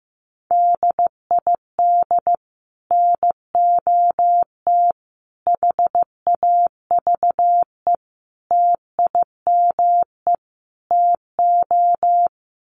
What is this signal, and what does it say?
Morse code, heard as a keyed tone, about 15 wpm, 700 Hz. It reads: DID NOT HAVE TIME TO
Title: Sets of 5 Words Encore 15wpm